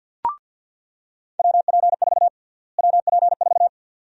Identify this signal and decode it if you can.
Morse code, heard as a keyed tone, about 49 wpm, 700 Hz. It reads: WP4 WP4
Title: USA Call Sign Prefixes - Repeated 2x - 50wpm